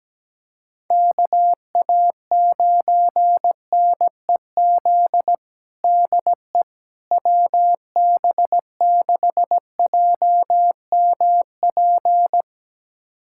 Morse code, heard as a keyed tone, about 17 wpm, 700 Hz. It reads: KA9NEZ DE WB6JMP
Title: USA Callsign DE Callsign 17wpm